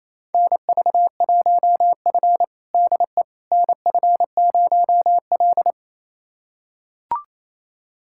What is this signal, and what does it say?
Morse code, heard as a keyed tone, about 28 wpm, 700 Hz. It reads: NV1F DE NF0L
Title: USA Callsign DE Callsign - 4 Characters - 28wpm